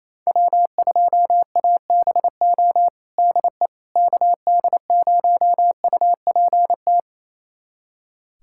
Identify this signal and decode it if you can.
Morse code, heard as a keyed tone, about 28 wpm, 700 Hz. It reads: W2ABO DE KD0UPT
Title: USA Callsign DE Callsign 28wpm